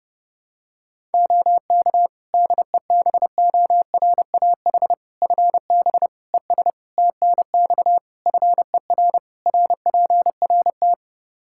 Morse code, heard as a keyed tone, about 30 wpm, 700 Hz. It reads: OK DEBORAH FB ES TNX FER RPRT
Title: Practice QSO - Exchange 2 - Info - 30wpm